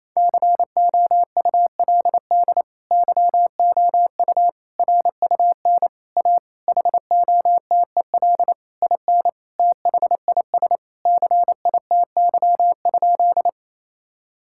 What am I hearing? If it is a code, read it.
Morse code, heard as a keyed tone, about 28 wpm, 700 Hz. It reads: COULD YOU RUN A HOTEL IN THIS CITY?
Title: Sentences from Top 1600 Words 28wpm